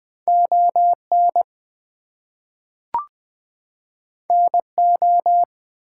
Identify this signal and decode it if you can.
Morse code, heard as a keyed tone, about 20 wpm, 700 Hz. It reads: ON NO
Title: Two Letter Words - 20wpm